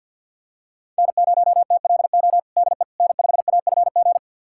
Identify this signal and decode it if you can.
Morse code, heard as a keyed tone, about 48 wpm, 700 Hz. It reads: N0TLO DE N5AFG
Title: USA Callsign DE Callsign - 5 Characters - 50wpm